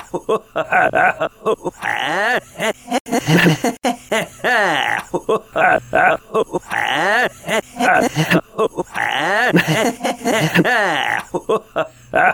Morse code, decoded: none